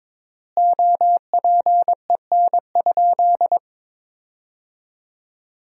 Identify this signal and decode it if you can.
Morse code, heard as a keyed tone, about 22 wpm, 700 Hz. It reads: OPEN?
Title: 3x Word-Spacing - Sentences from Top 300 Words - 22wpm